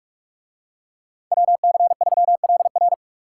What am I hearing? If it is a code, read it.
Morse code, heard as a keyed tone, about 44 wpm, 700 Hz. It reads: WC3LR